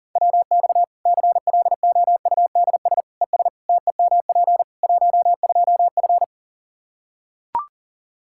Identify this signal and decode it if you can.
Morse code, heard as a keyed tone, about 39 wpm, 700 Hz. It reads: WX CLOUDS ES TEMP 12F
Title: Practice QSO - Exchange 2 - Info - 40wpm